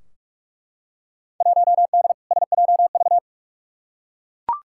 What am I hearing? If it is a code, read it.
Morse code, heard as a keyed tone, about 41 wpm, 700 Hz. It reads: 1D SJV
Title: Field Day Exchanges Encore - 45wpm